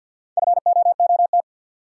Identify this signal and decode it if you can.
Morse code, heard as a keyed tone, about 52 wpm, 700 Hz. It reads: FOOT
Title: Sets of 5 Words - New Word Drill - 50wpm